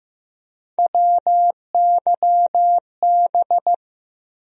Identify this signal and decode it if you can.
Morse code, heard as a keyed tone, about 15 wpm, 700 Hz. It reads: WYB
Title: USA Call Sign Postfixes - 3 Letters 15wpm